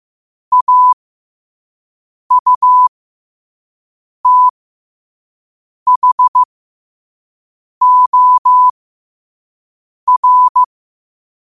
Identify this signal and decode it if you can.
Morse code, heard as a keyed tone, about 15 wpm, spaced as for 5 wpm, 1000 Hz. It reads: AUTHOR
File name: qotdfiles / 5wpm